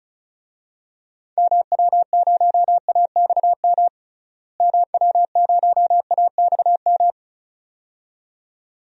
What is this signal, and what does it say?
Morse code, heard as a keyed tone, about 36 wpm, 700 Hz. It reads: MW0AXM MW0AXM
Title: DXCC-100 Call Signs - 6 Characters - Repeated 2x - 35wpm